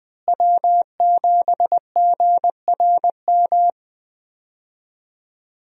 Morse code, heard as a keyed tone, about 20 wpm, 700 Hz. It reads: W7GRM
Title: USA Call Signs - 5 Characters - Repeated 2x - 20wpm